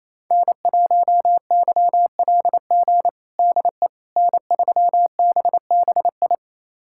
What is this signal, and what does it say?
Morse code, heard as a keyed tone, about 28 wpm, 700 Hz. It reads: N1YLG DE N3BBI
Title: USA Callsign DE Callsign - 5 Characters - 28wpm